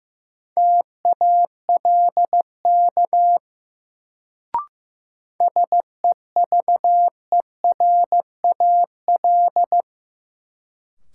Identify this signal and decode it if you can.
Morse code, heard as a keyed tone, about 15 wpm, 700 Hz. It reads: TALK SEVERAL
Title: Sentences from Top 400 Words - New Word Drill - 15wpm